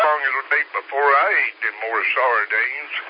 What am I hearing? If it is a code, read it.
Emergency Alert System (SAME) tones, heard as harsh buzzing bursts, with speech. It is heard from 0.2 to 2.9 s.